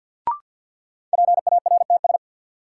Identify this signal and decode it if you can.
Morse code, heard as a keyed tone, about 53 wpm, 700 Hz. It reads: PARTS
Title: Sets of 3 Words - New Word Drill - 50wpm